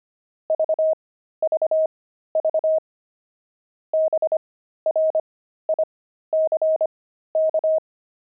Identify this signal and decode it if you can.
Morse code, heard as a keyed tone, about 25 wpm, spaced as for 13 wpm, 630 Hz. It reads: VVV BRICK
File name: pangram-fs-13-cs-25-veldt